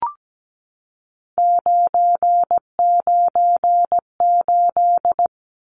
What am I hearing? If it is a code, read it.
Morse code, heard as a keyed tone, about 17 wpm, 700 Hz. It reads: 998
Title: Numbers - 3-Digit 17wpm